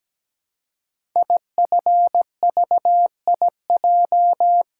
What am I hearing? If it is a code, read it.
Morse code, heard as a keyed tone, about 17 wpm, 700 Hz. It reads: IFVIJ
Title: Code Groups - 5 Letters - 17wpm